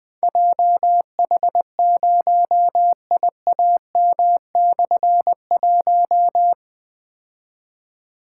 Morse code, heard as a keyed tone, about 20 wpm, 700 Hz. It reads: JH0IAM/1
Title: Difficult Call Signs 20wpm